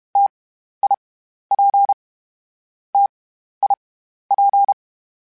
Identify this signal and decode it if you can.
Morse code, heard as a keyed tone, about 32 wpm, 800 Hz. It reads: T I P T I P
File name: Variable Speed / Variable Speed-mp3s / T AOEI's sp @30